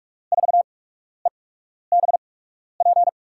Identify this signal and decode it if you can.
Morse code, heard as a keyed tone, about 42 wpm, spaced as for 12 wpm, 700 Hz. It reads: 4EBP
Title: USA Call Signs - 6 Characters - 8x Character Spacing - 45wpm